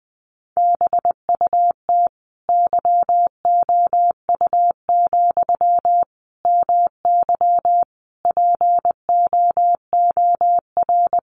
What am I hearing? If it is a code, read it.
Morse code, heard as a keyed tone, about 20 wpm, 700 Hz. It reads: BUT YOU, MY POOR
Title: Sentences from Top 1000 Words 20wpm